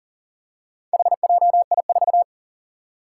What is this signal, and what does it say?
Morse code, heard as a keyed tone, about 40 wpm, 700 Hz. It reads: HJI4
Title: Four Challenging Letter-Number Combinations 40wpm